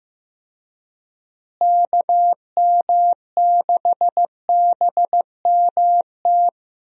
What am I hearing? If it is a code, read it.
Morse code, heard as a keyed tone, about 15 wpm, 700 Hz. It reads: KM6BMT
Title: USA Call Signs - 6 Characters - Repeated 3x - 15wpm